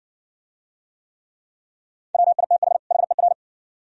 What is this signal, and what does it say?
Morse code, heard as a keyed tone, about 60 wpm, 700 Hz. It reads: WITH HER